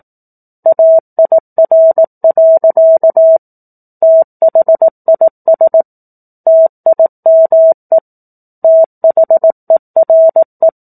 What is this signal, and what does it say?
Morse code, heard as a keyed tone, about 18 wpm, 650 Hz. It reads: AIR. THIS TIME THERE
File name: 04-Chapter-04 18 WPM